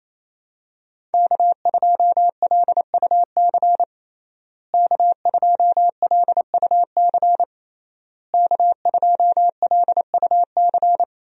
Morse code, heard as a keyed tone, about 28 wpm, 700 Hz. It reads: K2LUC K2LUC K2LUC
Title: USA Call Signs - 5 Characters - Repeated 3x - 28wpm